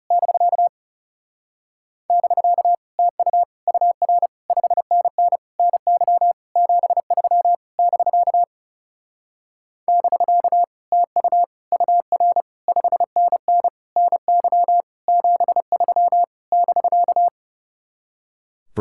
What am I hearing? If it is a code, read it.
Morse code, heard as a keyed tone, about 33 wpm, 700 Hz. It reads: <BK> <BK> TU UR 5NN NY 73 <BK> <BK> TU UR 5NN NY 73 <BK>
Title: Speed-Racing Common QSO Phrases - 30wpm